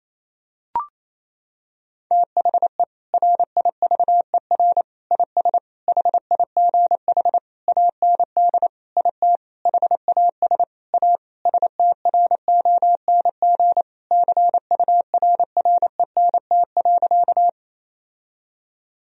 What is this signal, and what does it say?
Morse code, heard as a keyed tone, about 28 wpm, 700 Hz. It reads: THE RIVER IS HIGH AND IT HAS A STRONG CURRENT.
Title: Sentences from Top 1000 Words - Review Part 4 - 28wpm